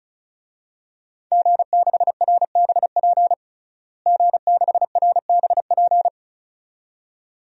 Morse code, heard as a keyed tone, about 36 wpm, 700 Hz. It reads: G6RBP G6RBP
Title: DXCC-100 Call Signs - 5 Characters - Repeated 2x - 35wpm